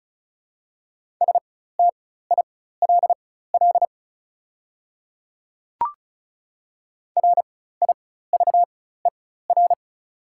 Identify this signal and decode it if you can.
Morse code, heard as a keyed tone, about 36 wpm, spaced as for 16 wpm, 690 Hz. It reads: STILL RIVER
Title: 4x Character Spacing - Top 500 Words - 35wpm